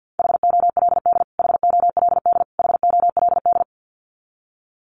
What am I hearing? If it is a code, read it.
Morse code, heard as a keyed tone, about 50 wpm, 700 Hz. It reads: HOLD HOLD HOLD
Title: Top 500 Words Repeated 50wpm